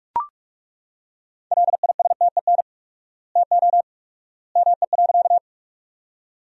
Morse code, heard as a keyed tone, about 42 wpm, 700 Hz. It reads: LISTEN TO ME.
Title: 4x Word-Spacing - Sentences from Top 400 Words - 45wpm